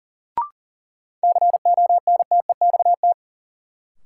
Morse code, heard as a keyed tone, about 39 wpm, 700 Hz. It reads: CONTEXT
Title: Sentences from Top 1000 Words - Review Part 7 - New Word Drill - 40wpm